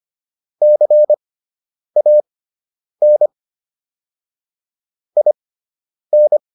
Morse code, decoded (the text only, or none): CAN IN